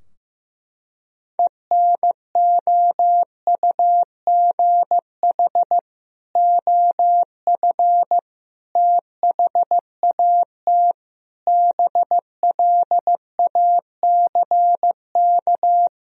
Morse code, heard as a keyed tone, about 15 wpm, 700 Hz. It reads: ENOUGH OF THAT BLACK
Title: Sentences from Top 1000 Words - Review Part 2 - 15wpm